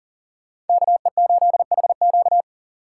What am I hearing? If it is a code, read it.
Morse code, heard as a keyed tone, about 52 wpm, 700 Hz. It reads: KE8HQ